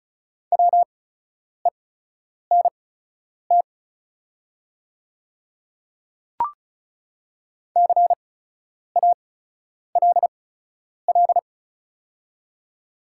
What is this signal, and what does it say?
Morse code, heard as a keyed tone, about 36 wpm, spaced as for 10 wpm, 690 Hz. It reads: WENT CALL
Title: 8x Character Spacing - Top 200 Words - 35wpm